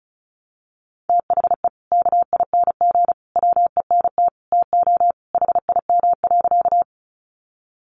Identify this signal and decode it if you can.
Morse code, heard as a keyed tone, about 34 wpm, 700 Hz. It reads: THE KING WENT TO HIM.
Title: Sentences from Top 400 Words 35wpm